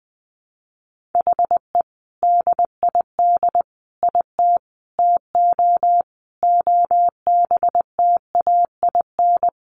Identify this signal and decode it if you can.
Morse code, heard as a keyed tone, about 20 wpm, 700 Hz. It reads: HE DID IT TO OBTAIN
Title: Sentences from Top 1500 Words 20wpm